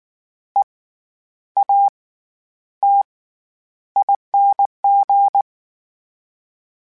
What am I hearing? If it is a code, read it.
Morse code, heard as a keyed tone, about 19 wpm, spaced as for 10 wpm, 800 Hz. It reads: EATING
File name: Special INGs / Special INGs-mp3s / SlowToFaster ing @18